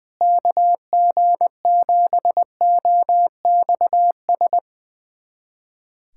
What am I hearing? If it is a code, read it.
Morse code, heard as a keyed tone, about 21 wpm, 700 Hz. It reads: KG7OXS
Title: USA Callsign DE Callsign 20wpm